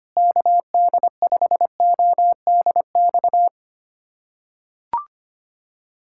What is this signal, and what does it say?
Morse code, heard as a keyed tone, about 25 wpm, 700 Hz. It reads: KD5ODX
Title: USA Callsign DE Callsign - 6 Characters - 25wpm